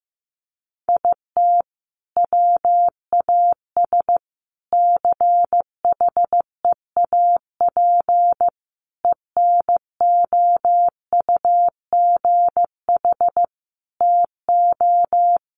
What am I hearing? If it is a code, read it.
Morse code, heard as a keyed tone, about 15 wpm, 700 Hz. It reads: IT WAS CHEAP ENOUGH TO
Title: Sentences from Top 1200 Words 15wpm